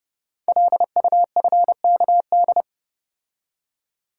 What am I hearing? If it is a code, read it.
Morse code, heard as a keyed tone, about 28 wpm, 690 Hz. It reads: LUFKD
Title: Code Groups - 5 Letters - 30wpm